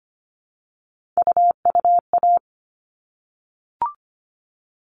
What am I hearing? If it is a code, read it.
Morse code, heard as a keyed tone, about 25 wpm, 700 Hz. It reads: UUA
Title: All Cut Numbers - 3-Digit 25wpm